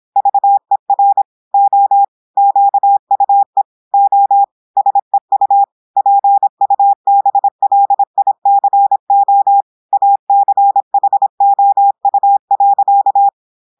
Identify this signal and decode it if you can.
Morse code, heard as a keyed tone, about 26 wpm, 800 Hz. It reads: VER O QUE O SEU PUBLICO ACHOU.
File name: portugais texte en clair 2 vitesse 26 mots minutes